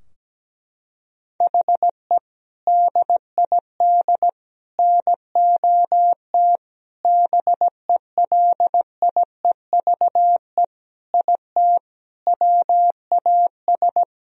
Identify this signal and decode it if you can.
Morse code, heard as a keyed tone, about 17 wpm, 700 Hz. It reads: HE DID NOT BELIEVE IT WAS
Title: Sentences from Top 1000 Words - Review Part 2 - 17wpm